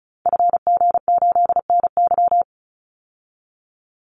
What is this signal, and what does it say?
Morse code, heard as a keyed tone, about 34 wpm, 700 Hz. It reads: FG8NY
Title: Call Signs Repeated 2x - 35wpm